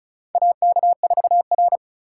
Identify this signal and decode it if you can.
Morse code, heard as a keyed tone, about 34 wpm, 700 Hz. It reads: AK4R
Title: USA Call Signs - 4 Characters - 4x Character Spacing - 35wpm